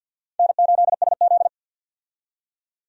Morse code, heard as a keyed tone, about 52 wpm, 700 Hz. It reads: N8SZ
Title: USA Call Signs - 4 Characters - 50wpm